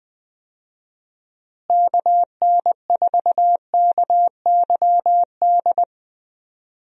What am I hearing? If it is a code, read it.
Morse code, heard as a keyed tone, about 21 wpm, 700 Hz. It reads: KN4KYD
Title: USA Call Signs - 6 Characters - Repeated 2x - 20wpm